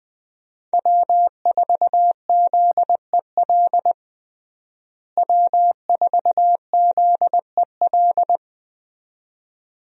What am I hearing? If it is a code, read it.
Morse code, heard as a keyed tone, about 20 wpm, 700 Hz. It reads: W4ZEL W4ZEL
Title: USA Call Signs - 5 Characters - Repeated 2x - 20wpm